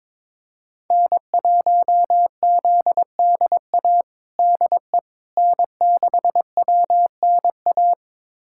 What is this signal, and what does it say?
Morse code, heard as a keyed tone, about 22 wpm, 700 Hz. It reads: N1ZDA DE N6WNA
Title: USA Callsign DE Callsign - 5 Characters - 22wpm